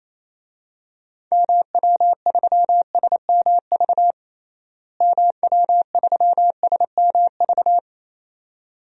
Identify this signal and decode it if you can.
Morse code, heard as a keyed tone, about 28 wpm, 700 Hz. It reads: MW3SMV MW3SMV
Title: DXCC-100 Call Signs - 6 Characters - Repeated 2x - 28wpm